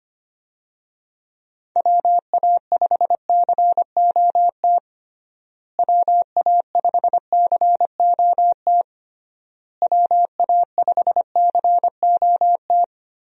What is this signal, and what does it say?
Morse code, heard as a keyed tone, about 25 wpm, 700 Hz. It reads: WA5COT WA5COT WA5COT
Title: USA Call Signs - 6 Characters - Repeated 3x - 25wpm